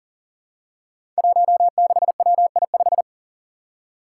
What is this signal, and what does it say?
Morse code, heard as a keyed tone, about 39 wpm, 700 Hz. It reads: 16WI5
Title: Five Challenging Letter-Number Combinations 40wpm